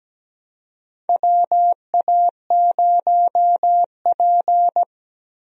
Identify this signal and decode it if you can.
Morse code, heard as a keyed tone, about 17 wpm, 700 Hz. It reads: WA0P
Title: USA Call Signs - 4 Characters - Repeated 2x - 17wpm